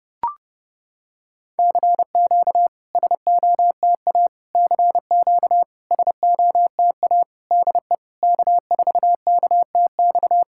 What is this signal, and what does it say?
Morse code, heard as a keyed tone, about 30 wpm, 700 Hz. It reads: CQ SOTA CQ SOTA DE K4KTX
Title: SOTA QSO Practice 30wpm